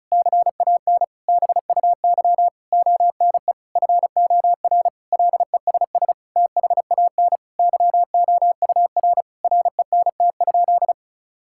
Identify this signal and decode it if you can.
Morse code, heard as a keyed tone, about 34 wpm, 700 Hz. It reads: CAN BUY ONE FOR LESS THAN YOUR RENT?
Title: Sentences from Top 1000 Words - Review Part 5 - 35wpm